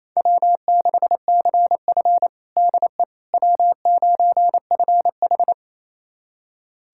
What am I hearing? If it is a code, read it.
Morse code, heard as a keyed tone, about 28 wpm, 700 Hz. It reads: W6CF DE W9FH
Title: USA Callsign DE Callsign - 4 Characters - 28wpm